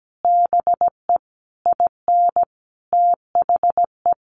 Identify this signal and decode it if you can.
Morse code, heard as a keyed tone, about 17 wpm, 700 Hz. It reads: BE IN THE
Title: Sentences from Top 1400 Words 17wpm